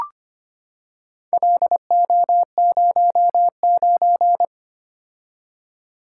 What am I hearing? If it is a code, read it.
Morse code, heard as a keyed tone, about 25 wpm, 700 Hz. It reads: LO09
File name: practice / 25wpm / Four Letter-Number Combinations - 25wpm